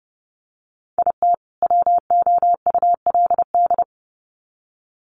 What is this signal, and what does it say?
Morse code, heard as a keyed tone, about 30 wpm, 700 Hz. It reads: IT WOULD
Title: Sets of 2 Words 30wpm